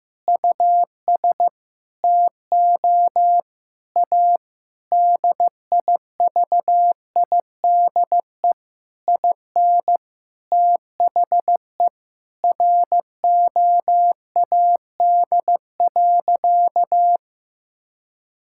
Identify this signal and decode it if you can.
Morse code, heard as a keyed tone, about 15 wpm, 700 Hz. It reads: US TO A DIVIDE IN THE ROAD.
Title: Sentences from Top 1600 Words 15wpm